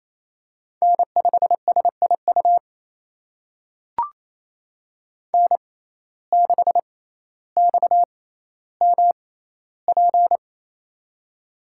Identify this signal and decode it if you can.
Morse code, heard as a keyed tone, about 28 wpm, spaced as for 14 wpm, 700 Hz. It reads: N5SIU N6XMP
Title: USA Call Signs - 5 Characters - 6x Character Spacing - 28wpm